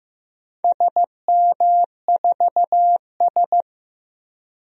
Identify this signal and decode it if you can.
Morse code, heard as a keyed tone, about 15 wpm, 700 Hz. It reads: SM4S
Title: DXCC-100 Call Signs - 4 Characters - 15wpm